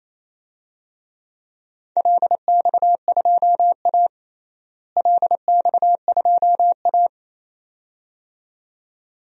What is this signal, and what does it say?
Morse code, heard as a keyed tone, about 27 wpm, 700 Hz. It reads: LX2A LX2A
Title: DXCC-100 Call Signs - 4 Characters - Repeated 2x - 28wpm